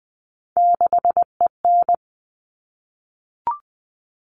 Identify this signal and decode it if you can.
Morse code, heard as a keyed tone, about 20 wpm, 700 Hz. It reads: 6EN
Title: All Cut Numbers - 3-Digit 20wpm